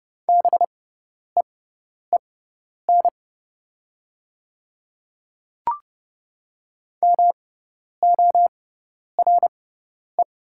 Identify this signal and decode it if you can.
Morse code, heard as a keyed tone, about 30 wpm, spaced as for 10 wpm, 700 Hz. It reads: BEEN MORE
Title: 6x Character Spacing - Top 100 Words - 30wpm